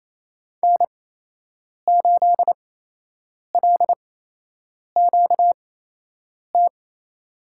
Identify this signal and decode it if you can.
Morse code, heard as a keyed tone, about 28 wpm, spaced as for 8 wpm, 700 Hz. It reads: N8LQT
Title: USA Call Signs - 8x Character Spacing - 28wpm